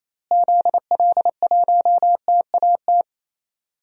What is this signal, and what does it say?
Morse code, heard as a keyed tone, about 28 wpm, 700 Hz. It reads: ZL1TAT
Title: DXCC-100 Call Signs - 6 Characters - Repeated 3x - 28wpm